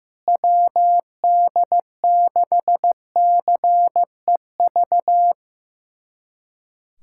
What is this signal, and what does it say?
Morse code, heard as a keyed tone, about 15 wpm, 700 Hz. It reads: WD6CEV
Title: USA Callsign DE Callsign - 6 Characters - 15wpm